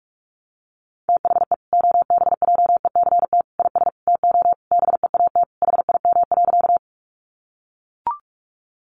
Morse code, heard as a keyed tone, about 44 wpm, 700 Hz. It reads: THE OBJECT IS TO BEAT HIM.
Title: Sentences from Top 1300 Words 45wpm